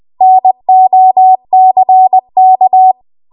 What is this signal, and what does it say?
Morse code, heard as a keyed tone, about 20 wpm, 750 Hz. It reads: NOCK
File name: Archive / 20 WPM / 20WPM